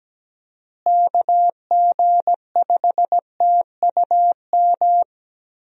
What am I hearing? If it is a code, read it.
Morse code, heard as a keyed tone, about 17 wpm, 700 Hz. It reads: KG5TUM
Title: USA Call Signs - 6 Characters - Repeated 3x - 17wpm